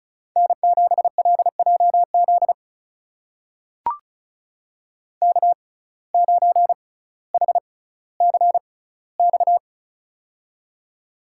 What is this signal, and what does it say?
Morse code, heard as a keyed tone, about 34 wpm, spaced as for 18 wpm, 700 Hz. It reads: N7LJZ K9HCX
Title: USA Call Signs - 5 Characters - 6x Character Spacing - 35wpm